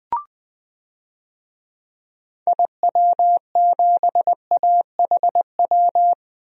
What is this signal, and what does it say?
Morse code, heard as a keyed tone, about 20 wpm, 700 Hz. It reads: IW7AHW
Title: DXCC-100 Call Signs - 6 Characters - 20wpm